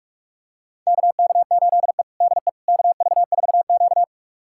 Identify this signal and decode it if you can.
Morse code, heard as a keyed tone, about 46 wpm, 700 Hz. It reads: KK8E DE KV4Q
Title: USA Callsign DE Callsign - 4 Characters - 45wpm